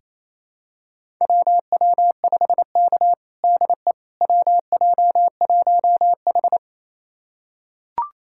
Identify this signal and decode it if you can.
Morse code, heard as a keyed tone, about 28 wpm, 700 Hz. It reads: WW5K DE WJ1H
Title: USA Callsign DE Callsign - 4 Characters - 28wpm